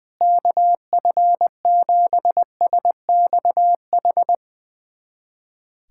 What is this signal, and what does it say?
Morse code, heard as a keyed tone, about 21 wpm, 700 Hz. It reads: KF7SXH
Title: USA Callsign DE Callsign - 6 Characters - 20wpm